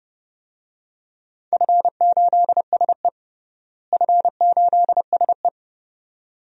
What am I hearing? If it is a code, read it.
Morse code, heard as a keyed tone, about 32 wpm, 700 Hz. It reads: F8SE F8SE
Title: DXCC-100 Call Signs - 4 Characters - Repeated 2x - 30wpm